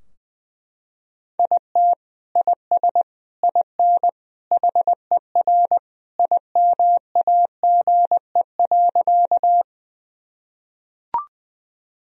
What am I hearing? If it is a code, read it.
Morse code, heard as a keyed tone, about 21 wpm, 700 Hz. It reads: IT IS IN HER IMAGE.
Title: Sentences from Top 1000 Words - Review Part 4 - 20wpm